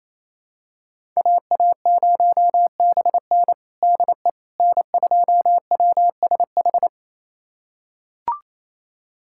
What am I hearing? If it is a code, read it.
Morse code, heard as a keyed tone, about 28 wpm, 700 Hz. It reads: AA0BN DE N2WSH
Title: USA Callsign DE Callsign - 5 Characters - 28wpm